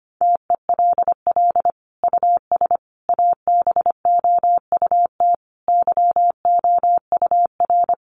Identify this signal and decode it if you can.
Morse code, heard as a keyed tone, about 25 wpm, 700 Hz. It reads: TELL US ABOUT YOUR
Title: Sentences from Top 1200 Words 25wpm